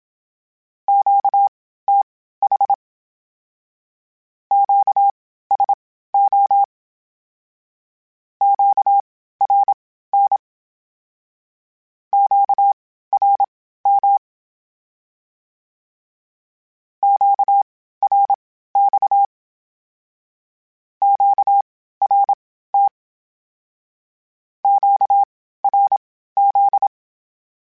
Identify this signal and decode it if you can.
Morse code, heard as a keyed tone, about 27 wpm, spaced as for 15 wpm, 800 Hz. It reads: QTH QSO QRN QRM QRX QRT QRZ
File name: (spaced) / Q-Calls Spaced-mp3s / Q Calls sp Once @25